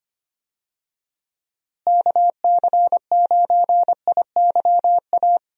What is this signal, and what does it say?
Morse code, heard as a keyed tone, about 25 wpm, 700 Hz. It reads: KC9IYA
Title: USA Call Signs - 6 Characters - Repeated 3x - 25wpm